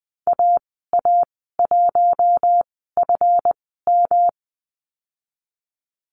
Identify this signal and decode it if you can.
Morse code, heard as a keyed tone, about 20 wpm, spaced as for 15 wpm, 700 Hz. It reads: AA1FM
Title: 154 Lesson 25g - Letter p - Call Signs 20wpm